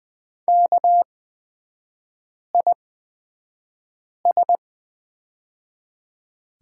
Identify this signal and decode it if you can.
Morse code, heard as a keyed tone, about 20 wpm, spaced as for 5 wpm, 690 Hz. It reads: KIS
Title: 229 Lesson 37e - Letter x - Three Characters 20wpm